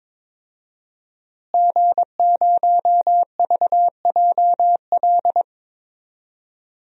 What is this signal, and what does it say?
Morse code, heard as a keyed tone, about 22 wpm, 700 Hz. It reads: G0VJL
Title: DXCC-100 Call Signs - 5 Characters - Repeated 3x - 22wpm